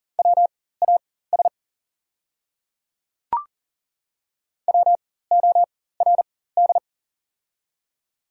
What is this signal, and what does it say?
Morse code, heard as a keyed tone, about 40 wpm, spaced as for 19 wpm, 690 Hz. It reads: WAS WORD
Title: 4x Character Spacing - Top 100 Words - 40wpm